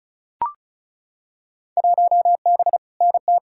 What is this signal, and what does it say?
Morse code, heard as a keyed tone, about 36 wpm, 700 Hz. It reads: 1B NT
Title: Field Day Exchanges Encore - 35wpm